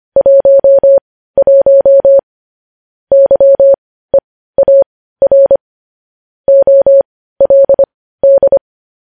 Morse code, heard as a keyed tone, about 25 wpm, spaced as for 15 wpm, 550 Hz. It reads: 11 YEAR OLD
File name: podcast / News headlines-1x-25x15